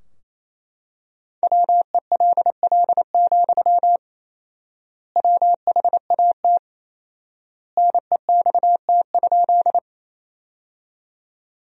Morse code, heard as a keyed tone, about 27 wpm, 700 Hz. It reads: WELL, WHAT NEXT?
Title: 4x Word-Spacing - Sentences from Top 300 Words - 28wpm